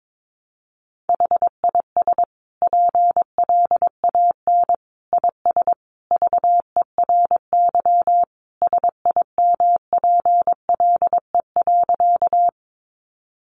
Morse code, heard as a keyed tone, about 22 wpm, 700 Hz. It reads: HIS PLAN IS VERY SIMPLE.
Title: Sentences from Top 500 Words 22wpm